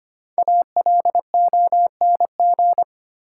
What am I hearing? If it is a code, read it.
Morse code, heard as a keyed tone, about 25 wpm, 700 Hz. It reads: ALONG
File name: 5 Letter Words - First 3 Letters Given - 25wpm